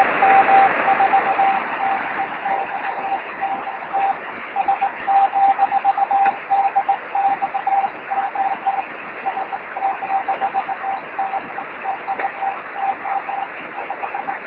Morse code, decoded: none

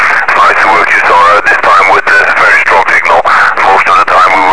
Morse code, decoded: none

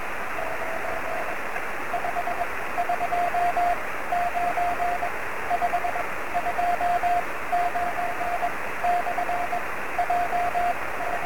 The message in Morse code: none

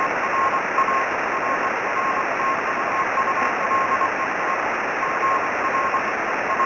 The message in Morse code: none

